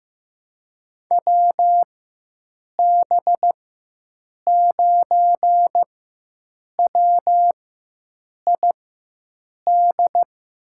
Morse code, heard as a keyed tone, about 15 wpm, spaced as for 7 wpm, 700 Hz. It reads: WB9WID
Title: USA Call Signs - 4x Character Spacing - 15wpm